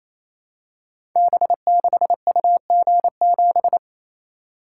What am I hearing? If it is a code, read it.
Morse code, heard as a keyed tone, about 29 wpm, 700 Hz. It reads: B6UG7
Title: Code Groups - 5 Letter:Numbers - 28wpm